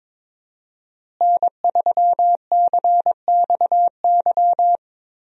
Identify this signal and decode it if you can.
Morse code, heard as a keyed tone, about 22 wpm, 700 Hz. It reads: N3CXY
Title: USA Call Signs - 5 Characters - Repeated 3x - 22wpm